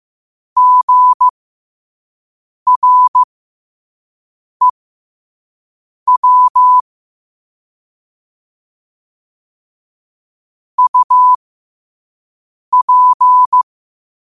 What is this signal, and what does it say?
Morse code, heard as a keyed tone, about 15 wpm, spaced as for 5 wpm, 1000 Hz. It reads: GREW UP
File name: qotdfiles / 5wpm